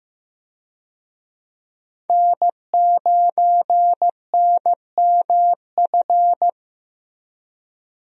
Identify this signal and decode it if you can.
Morse code, heard as a keyed tone, about 15 wpm, 700 Hz. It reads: N9NMF
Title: USA Call Signs - 5 Characters - Repeated 2x - 15wpm